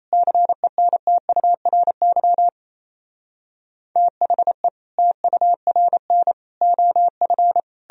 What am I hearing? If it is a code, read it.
Morse code, heard as a keyed tone, about 37 wpm, 700 Hz. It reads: CENTURY THE TURN OF